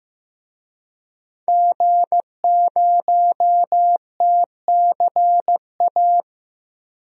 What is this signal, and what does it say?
Morse code, heard as a keyed tone, about 15 wpm, 700 Hz. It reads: G0TCA
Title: DXCC-100 Call Signs - 5 Characters - 15wpm